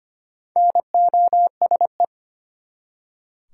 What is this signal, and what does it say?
Morse code, heard as a keyed tone, about 25 wpm, 700 Hz. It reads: NOSE
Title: Four Letter Words - 25wpm